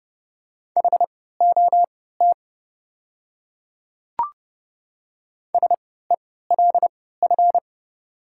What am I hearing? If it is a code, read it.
Morse code, heard as a keyed tone, about 31 wpm, spaced as for 17 wpm, 700 Hz. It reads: HOT SELF
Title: 3x Character Spacing - Top 200 Words - 30wpm